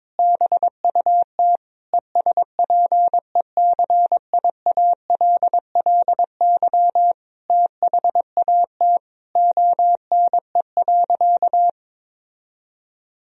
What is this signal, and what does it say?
Morse code, heard as a keyed tone, about 22 wpm, 700 Hz. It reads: BUT ESPECIALLY THAT ONE.
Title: Sentences from Top 1000 Words - Review Part 4 - 22wpm